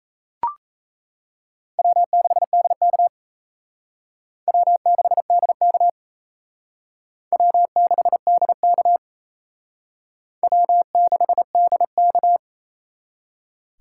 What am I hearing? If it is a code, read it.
Morse code, heard as a keyed tone, about 34 wpm, 700 Hz. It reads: W6DK W6DK W6DK W6DK
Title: Speed Racing - USA 1x2 Call Signs 28wpm